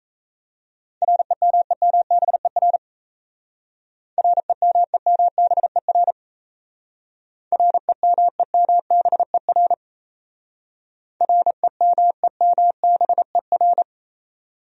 Morse code, heard as a keyed tone, about 33 wpm, 700 Hz. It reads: REMEMBER REMEMBER REMEMBER REMEMBER
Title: Speed-Racing Top 500 Words 28wpm